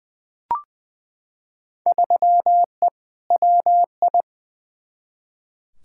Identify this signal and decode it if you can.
Morse code, heard as a keyed tone, about 20 wpm, 700 Hz. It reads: 3E WI
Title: Field Day Exchanges Encore - 20wpm